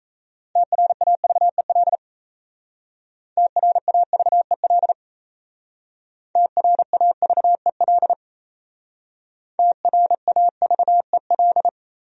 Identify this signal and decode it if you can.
Morse code, heard as a keyed tone, about 36 wpm, 700 Hz. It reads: TRAVEL TRAVEL TRAVEL TRAVEL